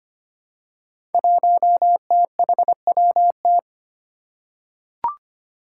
Morse code, heard as a keyed tone, about 25 wpm, 700 Hz. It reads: 1THWT
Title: Five Challenging Letter-Number Combinations 25wpm